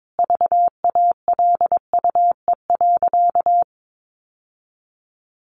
Morse code, heard as a keyed tone, about 22 wpm, 700 Hz. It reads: VALUE.
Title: Sentences from Top 1100 Words 22wpm